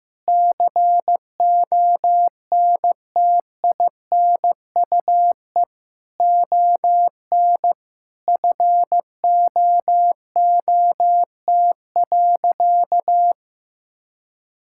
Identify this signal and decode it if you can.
Morse code, heard as a keyed tone, about 15 wpm, 700 Hz. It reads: CONTINUE ON FOOT.
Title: Sentences from Top 1000 Words - Review Part 6 - 15wpm